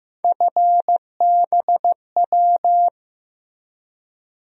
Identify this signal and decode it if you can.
Morse code, heard as a keyed tone, about 15 wpm, 700 Hz. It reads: FBW
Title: USA Call Sign Postfixes - 3 Letters 15wpm